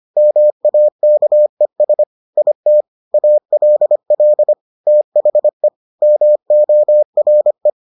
Morse code, heard as a keyed tone, about 25 wpm, 600 Hz. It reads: MAKES IT ALL THE MORE